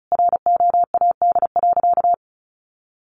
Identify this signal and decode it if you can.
Morse code, heard as a keyed tone, about 34 wpm, 700 Hz. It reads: ROAD.
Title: Sentences from Top 400 Words 35wpm